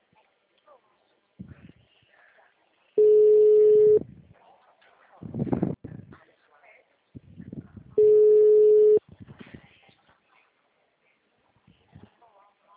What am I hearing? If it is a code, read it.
Ringback tone (Europe, 425 Hz), with speech.